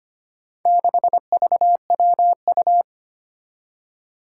Morse code, heard as a keyed tone, about 25 wpm, 700 Hz. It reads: 6VWU
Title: Four Challenging Letter-Number Combinations 25wpm